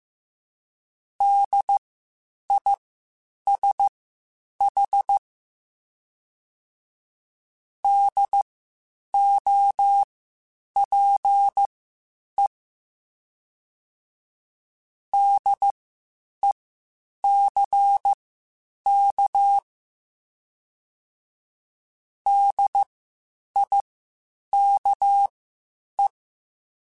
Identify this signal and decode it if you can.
Morse code, heard as a keyed tone, about 15 wpm, spaced as for 8 wpm, 800 Hz. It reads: DISH DOPE DECK DIKE
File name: Spaced Alphas / Spaced Alphas mp3s / Ds sp @13